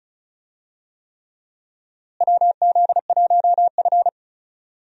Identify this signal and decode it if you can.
Morse code, heard as a keyed tone, about 36 wpm, 700 Hz. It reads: WZ1F
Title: USA Call Signs - 4 Characters - Repeated 3x - 35wpm